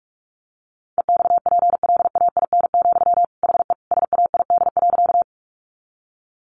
Morse code, heard as a keyed tone, about 44 wpm, 700 Hz. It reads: EXPLAIN, HE SAID.
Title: Sentences from Top 1100 Words 45wpm